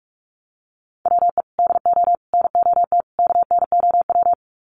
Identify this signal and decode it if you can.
Morse code, heard as a keyed tone, about 48 wpm, 700 Hz. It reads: WE DO NOT KNOW
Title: Sets of 4 Words 45wpm